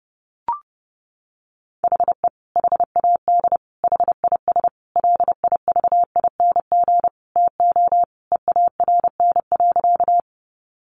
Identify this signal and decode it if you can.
Morse code, heard as a keyed tone, about 30 wpm, 700 Hz. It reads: HE HAD HIS LIVING TO EARN.
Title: Sentences from Top 1500 Words 30wpm